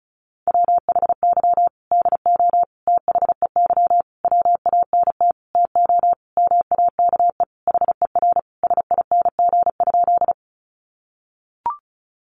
Morse code, heard as a keyed tone, about 34 wpm, 700 Hz. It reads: WHY DO THEY WANT TO MAKE HER SING?
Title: Sentences from Top 400 Words 35wpm